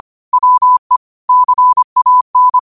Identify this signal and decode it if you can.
Morse code, heard as a keyed tone, about 25 wpm, 1000 Hz. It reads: WE CAN
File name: qotdfiles / 25wpm